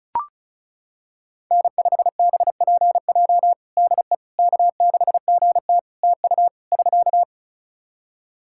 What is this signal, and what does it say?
Morse code, heard as a keyed tone, about 34 wpm, 700 Hz. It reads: N5BPJ DE K6GT TU <SK>
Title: K1USN SST QSO Practice 35wpm